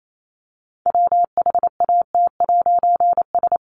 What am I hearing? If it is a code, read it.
Morse code, heard as a keyed tone, about 28 wpm, 700 Hz. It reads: WHAT'S
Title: A Sound Of Thunder 28wpm